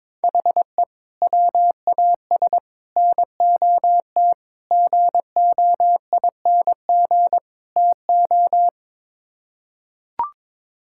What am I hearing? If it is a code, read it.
Morse code, heard as a keyed tone, about 22 wpm, 700 Hz. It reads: HE WAS NOT GOING TO
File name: practice / 22wpm / Build A Phrase - 5-Word Phrases - 22wpm